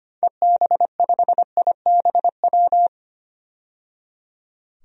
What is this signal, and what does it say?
Morse code, heard as a keyed tone, about 25 wpm, 700 Hz. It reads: EB5IBW
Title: DXCC-100 Call Signs - 6 Characters - 25wpm